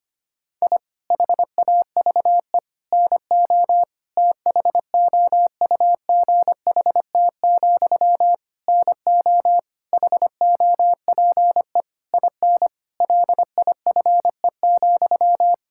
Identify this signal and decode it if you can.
Morse code, heard as a keyed tone, about 25 wpm, 700 Hz. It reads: I HAVE NO THOUGHT, NO HOPE IN LIFE,
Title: Sentences from Top 1000 Words - Review Part 6 - 25wpm